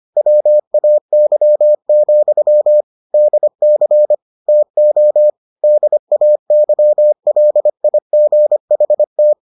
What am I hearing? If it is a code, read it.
Morse code, heard as a keyed tone, about 25 wpm, 600 Hz. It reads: WAY, DC TO DAYLIGHT